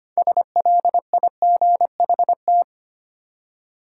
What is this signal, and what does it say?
Morse code, heard as a keyed tone, about 25 wpm, 700 Hz. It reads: SLIGHT
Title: Six Letter Words - 25wpm